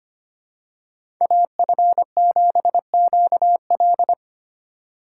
Morse code, heard as a keyed tone, about 25 wpm, 700 Hz. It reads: AF7QL
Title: USA Call Signs - 5 Characters - 4x Character Spacing - 25wpm